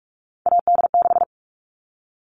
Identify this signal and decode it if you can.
Morse code, heard as a keyed tone, about 44 wpm, 700 Hz. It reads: AD6
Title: All Cut Numbers - 3-Digit 45wpm